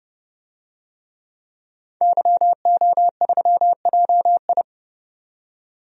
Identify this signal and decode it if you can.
Morse code, heard as a keyed tone, about 29 wpm, 700 Hz. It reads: YO3JI
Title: DXCC-100 Call Signs - 5 Characters - 30wpm